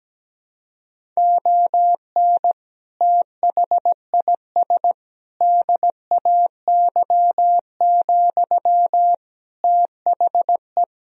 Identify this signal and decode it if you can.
Morse code, heard as a keyed tone, about 17 wpm, 700 Hz. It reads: ON THIS DAY, THE
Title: Sentences from Top 1000 Words - Review Part 2 - 17wpm